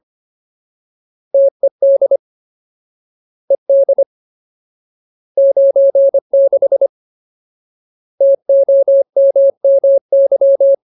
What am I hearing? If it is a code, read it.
Morse code, heard as a keyed tone, about 25 wpm, 550 Hz. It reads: TED ED 96 TOMMY
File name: Ultimate Word List / 25WPM Ultimate